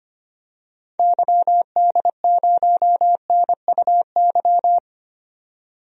Morse code, heard as a keyed tone, about 25 wpm, 700 Hz. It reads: YD0NUY
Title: DXCC-100 Call Signs - 6 Characters - 25wpm